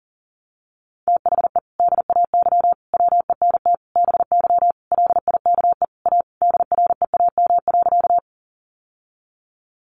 Morse code, heard as a keyed tone, about 40 wpm, 700 Hz. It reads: THE DAY WENT BY LIKE A DREAM.
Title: Sentences from Top 1500 Words 40wpm